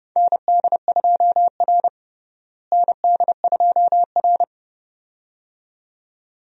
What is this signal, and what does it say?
Morse code, heard as a keyed tone, about 30 wpm, 700 Hz. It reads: ND2R ND2R
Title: USA Call Signs - 4 Characters - Repeated 2x - 30wpm